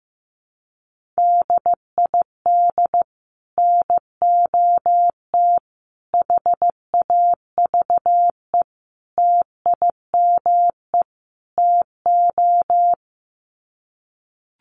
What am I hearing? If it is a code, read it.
Morse code, heard as a keyed tone, about 15 wpm, 700 Hz. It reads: DID NOT HAVE TIME TO
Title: Sets of 5 Words Encore 15wpm